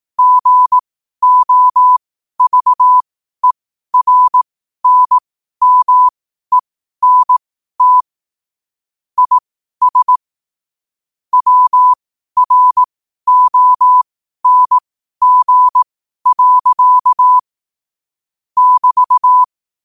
Morse code, heard as a keyed tone, about 18 wpm, spaced as for 12 wpm, 1000 Hz. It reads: GOVERNMENT IS WRONG. <BT>
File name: qotdfiles / 12wpm